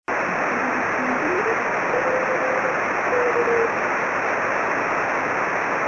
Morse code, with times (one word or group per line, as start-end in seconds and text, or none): none